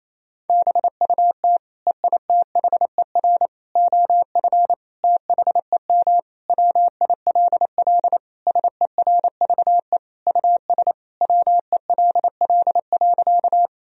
Morse code, heard as a keyed tone, about 28 wpm, 700 Hz. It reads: BUT EITHER OF THEM WILL SERVE US WELL.
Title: Sentences from Top 1000 Words - Review Part 4 - 28wpm